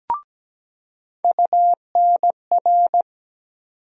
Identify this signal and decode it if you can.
Morse code, heard as a keyed tone, about 17 wpm, 700 Hz. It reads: UNR
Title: USA Call Sign Postfixes - 3 Letters 17wpm